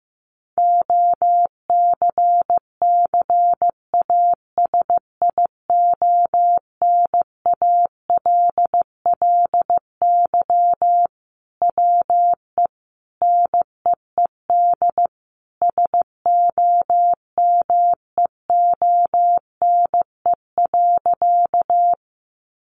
Morse code, heard as a keyed tone, about 15 wpm, 700 Hz. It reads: OCCASIONALLY WE NEED SOMEONE.
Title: Sentences from Top 1300 Words 15wpm